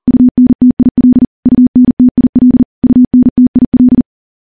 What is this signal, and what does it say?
Morse code, heard as a keyed tone, about 40 wpm, 250 Hz. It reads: UNTIL UNTIL UNTIL
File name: morse / morse-training-40WPM